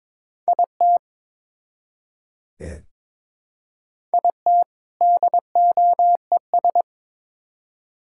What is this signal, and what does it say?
Morse code, heard as a keyed tone, about 21 wpm, 700 Hz. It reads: IT IT DOES
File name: Build A Phrase - 2-Word Phrases - 22wpm